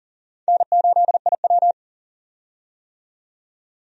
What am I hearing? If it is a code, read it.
Morse code, heard as a keyed tone, about 37 wpm, 700 Hz. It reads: N8IW
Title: USA Call Signs - 4 Characters - 40wpm